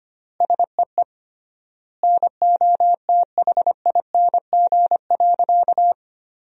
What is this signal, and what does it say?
Morse code, heard as a keyed tone, about 25 wpm, 700 Hz. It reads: SEE NOTHING.
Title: 3x Word-Spacing - Sentences from Top 500 Words - 25wpm